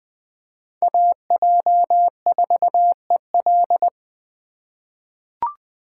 Morse code, heard as a keyed tone, about 20 wpm, 700 Hz. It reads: AJ4EL
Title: USA Call Signs - 5 Characters - 4x Character Spacing - 20wpm